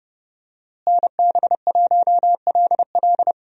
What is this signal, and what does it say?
Morse code, heard as a keyed tone, about 30 wpm, 700 Hz. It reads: NB1LL
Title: USA Call Signs - 5 Characters - Repeated 3x - 30wpm